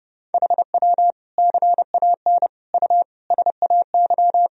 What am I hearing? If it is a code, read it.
Morse code, heard as a keyed tone, about 30 wpm, 700 Hz. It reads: HW CAN U SAY
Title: Sentences from Top 200 Words and Abbreviations 30wpm